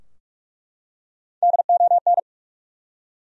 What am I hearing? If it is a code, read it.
Morse code, heard as a keyed tone, about 46 wpm, 700 Hz. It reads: DON
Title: USA Common Names - Max 3 Letters - 45wpm